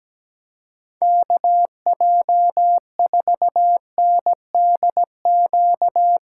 Morse code, heard as a keyed tone, about 18 wpm, 700 Hz. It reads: KJ4NDQ
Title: Speed Racing - USA Call Signs 17wpm